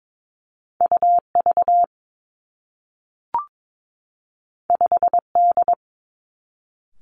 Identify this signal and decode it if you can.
Morse code, heard as a keyed tone, about 22 wpm, 700 Hz. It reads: U V 5 D
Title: Two Challenging Letter-Number Combinations 22wpm